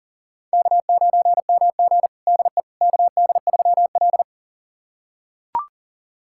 Morse code, heard as a keyed tone, about 40 wpm, 700 Hz. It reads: K9MG DE KD3L
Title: USA Callsign DE Callsign - 4 Characters - 40wpm